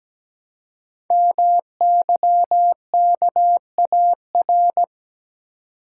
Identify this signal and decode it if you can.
Morse code, heard as a keyed tone, about 17 wpm, 700 Hz. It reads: MYKAR
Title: Code Groups - 5 Letters - 17wpm